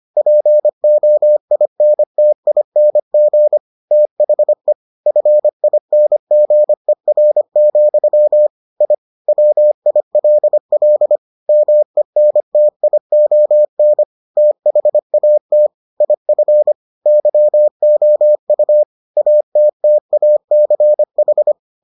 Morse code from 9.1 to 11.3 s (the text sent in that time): WILL